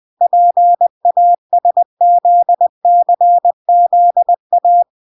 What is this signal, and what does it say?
Morse code, heard as a keyed tone, about 20 wpm, 700 Hz. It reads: PASZCZA